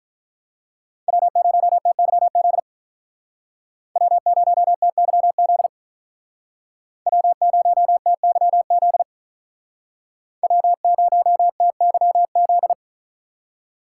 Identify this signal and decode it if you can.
Morse code, heard as a keyed tone, about 44 wpm, 700 Hz. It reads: W0TYZ W0TYZ W0TYZ W0TYZ